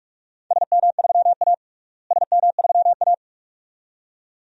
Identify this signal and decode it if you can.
Morse code, heard as a keyed tone, about 46 wpm, 700 Hz. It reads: SM3A SM3A
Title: DXCC-100 Call Signs - 4 Characters - Repeated 2x - 45wpm